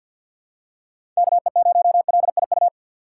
Morse code, heard as a keyed tone, about 48 wpm, 700 Hz. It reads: KE0LIU
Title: USA Call Signs - 6x Character Spacing - 50wpm